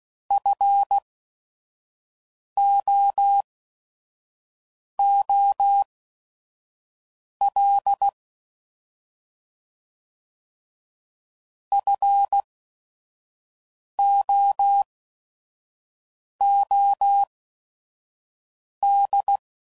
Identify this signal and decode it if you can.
Morse code, heard as a keyed tone, about 16 wpm, spaced as for 5 wpm, 800 Hz. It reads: FOOL FOOD
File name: Prefixes / By First Letter / Fs